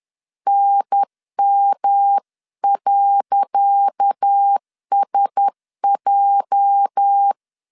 Morse code, heard as a keyed tone, about 10 wpm, 800 Hz. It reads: NM.SJ
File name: LEZIONI / 10 SIGNS / 10C DOT 10wpm 2940